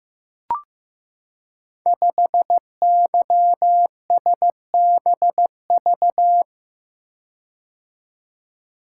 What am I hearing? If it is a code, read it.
Morse code, heard as a keyed tone, about 15 wpm, 700 Hz. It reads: 5YSBV
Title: Code Groups - 5 Letter:Numbers - 15wpm